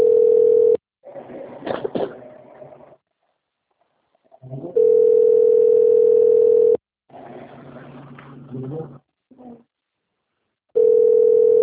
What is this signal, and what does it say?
Ringback tone (North America), 440 + 480 Hz.
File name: rg-1600-unknown- - - .98981